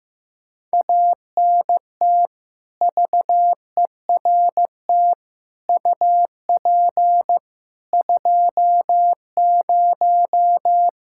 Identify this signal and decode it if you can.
Morse code, heard as a keyed tone, about 15 wpm, 700 Hz. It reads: ANT VERT UP 20
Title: K1USN SST QSO Practice 15wpm